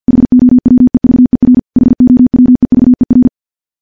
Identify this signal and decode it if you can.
Morse code, heard as a keyed tone, about 50 wpm, 250 Hz. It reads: HOWEVER HOWEVER
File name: morse-training-50WPM